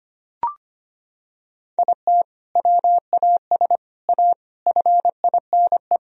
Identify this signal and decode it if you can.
Morse code, heard as a keyed tone, about 25 wpm, 700 Hz. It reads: IT WAS A FINE
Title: Sentences from Top 1000 Words - Review Part 3 - 25wpm